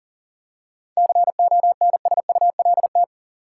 Morse code, heard as a keyed tone, about 40 wpm, 700 Hz. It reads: CONSULT
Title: Seven Letter Words - 40wpm